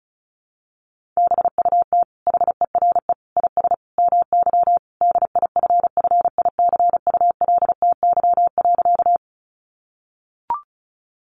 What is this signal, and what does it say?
Morse code, heard as a keyed tone, about 34 wpm, 700 Hz. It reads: BUT HERE IS MY DIFFICULTY.
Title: Sentences from Top 1500 Words 35wpm